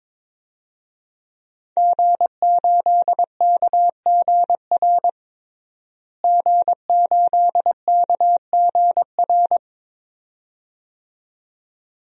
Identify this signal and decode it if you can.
Morse code, heard as a keyed tone, about 21 wpm, 700 Hz. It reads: G8KGR G8KGR
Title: DXCC-100 Call Signs - 5 Characters - Repeated 2x - 22wpm